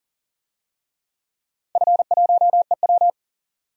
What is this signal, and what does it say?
Morse code, heard as a keyed tone, about 40 wpm, 700 Hz. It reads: F1EW
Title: DXCC-100 Call Signs - 4 Characters - 40wpm